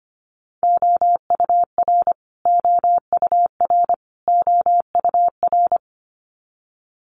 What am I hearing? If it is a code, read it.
Morse code, heard as a keyed tone, about 25 wpm, 700 Hz. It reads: OUR OUR OUR
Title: Top 200 Words Repeated 25wpm